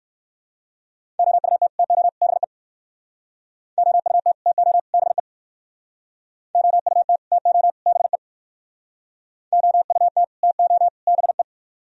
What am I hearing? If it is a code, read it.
Morse code, heard as a keyed tone, about 57 wpm, 700 Hz. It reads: OUT TO BE OUT TO BE OUT TO BE OUT TO BE